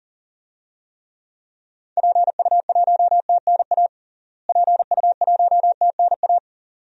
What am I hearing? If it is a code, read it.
Morse code, heard as a keyed tone, about 40 wpm, 700 Hz. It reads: PU1TNA PU1TNA
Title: DXCC-100 Call Signs - 6 Characters - Repeated 2x - 40wpm